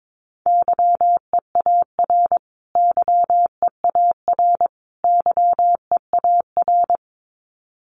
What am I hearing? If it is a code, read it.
Morse code, heard as a keyed tone, about 22 wpm, 700 Hz. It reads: YEAR YEAR YEAR
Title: Top 200 Words Repeated 22wpm